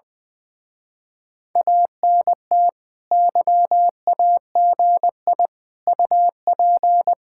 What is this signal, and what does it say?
Morse code, heard as a keyed tone, about 21 wpm, 700 Hz. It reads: ANT YAGI UP
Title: Speed-Racing Common QSO Phrases - 20wpm